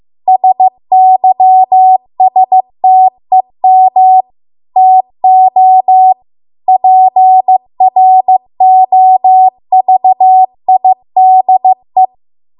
Morse code, heard as a keyed tone, about 15 wpm, 750 Hz. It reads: SYSTEM TO PROVIDE